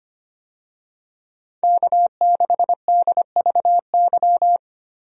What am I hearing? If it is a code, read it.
Morse code, heard as a keyed tone, about 25 wpm, 700 Hz. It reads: K6DVY
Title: USA Call Signs - 5 Characters - Repeated 3x - 25wpm